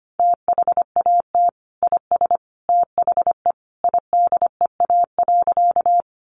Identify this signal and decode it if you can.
Morse code, heard as a keyed tone, about 25 wpm, 700 Hz. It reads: THAT IS THE IDEA.
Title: Sentences from Top 300 Words 25wpm